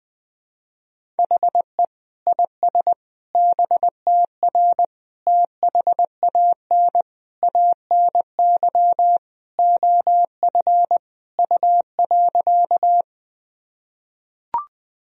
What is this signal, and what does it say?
Morse code, heard as a keyed tone, about 20 wpm, 700 Hz. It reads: HE IS BTR THAN ANY OF U.
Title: Sentences from Top 400 Words and Abbreviations 20wpm